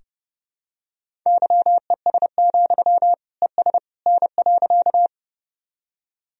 Morse code, heard as a keyed tone, about 30 wpm, 700 Hz. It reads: YES, ES N.
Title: Sentences from Top 500 Words and Abbreviations 30wpm